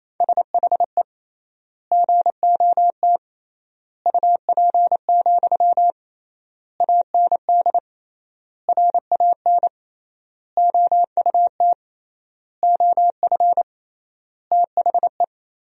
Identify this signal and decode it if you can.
Morse code, heard as a keyed tone, about 28 wpm, 700 Hz. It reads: SHE GOT UP, AND RAN OUT OF THE
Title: 3x Word-Spacing - Sentences from Top 500 Words - 28wpm